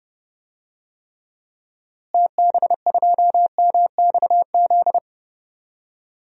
Morse code, heard as a keyed tone, about 31 wpm, 700 Hz. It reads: TB2MXZ
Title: DXCC-100 Call Signs - 6 Characters - Repeated 2x - 30wpm